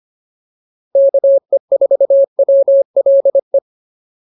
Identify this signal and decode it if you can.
Morse code, heard as a keyed tone, about 25 wpm, 590 Hz. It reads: KE4WLE